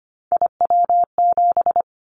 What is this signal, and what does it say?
Morse code, heard as a keyed tone, about 25 wpm, 700 Hz. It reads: IW7
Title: Callsign Prefixes for Top 50 DXCC - 25wpm